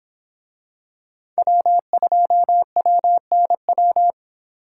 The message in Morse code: W2WNW

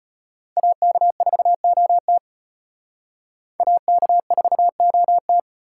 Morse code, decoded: AK4OT AK4OT

